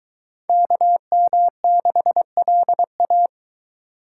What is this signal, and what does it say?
Morse code, heard as a keyed tone, about 23 wpm, 700 Hz. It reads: KM6LA